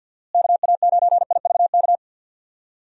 Morse code, heard as a keyed tone, about 50 wpm, 700 Hz. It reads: KA9IVK
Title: USA Call Signs - 6 Characters - Repeated 2x - 50wpm